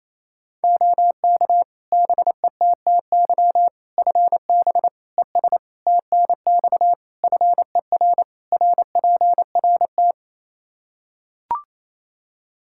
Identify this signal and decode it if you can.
Morse code, heard as a keyed tone, about 28 wpm, 700 Hz. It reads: OK BETTY FB ES TNX FER RPRT
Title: Practice QSO - Exchange 2 - Info - 28wpm